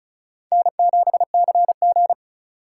Morse code, heard as a keyed tone, about 34 wpm, 700 Hz. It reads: N7CG
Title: USA Call Signs - 4 Characters - 3x Character Spacing - 35wpm